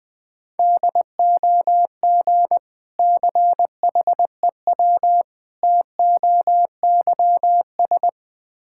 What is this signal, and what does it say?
Morse code, heard as a keyed tone, about 20 wpm, 700 Hz. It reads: DOG CHEW TOYS
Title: Difficult Words with Hints - 20wpm